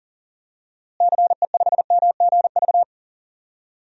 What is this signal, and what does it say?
Morse code, heard as a keyed tone, about 37 wpm, 700 Hz. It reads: CE5MGV
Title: DXCC-100 Call Signs - 6 Characters - Repeated 3x - 40wpm